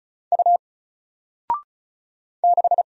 Morse code, heard as a keyed tone, about 36 wpm, 700 Hz. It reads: U 6
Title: Single Letter-Number Alt 35wpm